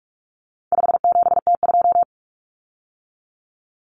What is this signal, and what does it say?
Morse code, heard as a keyed tone, about 44 wpm, 700 Hz. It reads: 57T2
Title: Common Cut Numbers - 4-Digit 45wpm